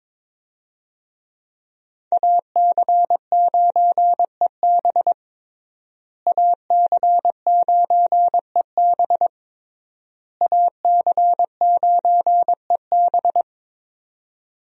Morse code, heard as a keyed tone, about 22 wpm, 700 Hz. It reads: AC9EB AC9EB AC9EB
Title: USA Call Signs - 5 Characters - Repeated 3x - 22wpm